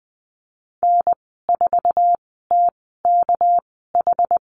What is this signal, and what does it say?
Morse code, heard as a keyed tone, about 19 wpm, spaced as for 14 wpm, 700 Hz. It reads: N4TKH
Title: 238 Lesson 38g - Letter q - Call Signs 20wpm